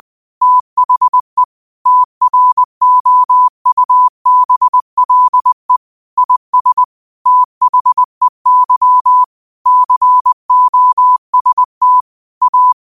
Morse code, heard as a keyed tone, about 20 wpm, 1000 Hz. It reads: THE TROUBLE IS THEY COST A
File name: qotdfiles / 20wpm